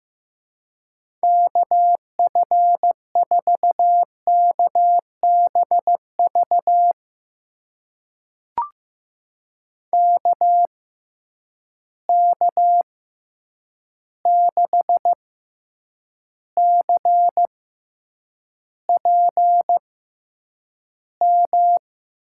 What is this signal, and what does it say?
Morse code, heard as a keyed tone, about 15 wpm, spaced as for 8 wpm, 700 Hz. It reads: KF4KBV KK6CPM
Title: USA Call Signs - 6 Characters - 6x Character Spacing - 15wpm